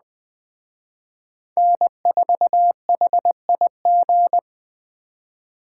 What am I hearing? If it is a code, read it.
Morse code, heard as a keyed tone, about 21 wpm, 700 Hz. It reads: N4HIG
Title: Speed Racing - USA Call Signs 17wpm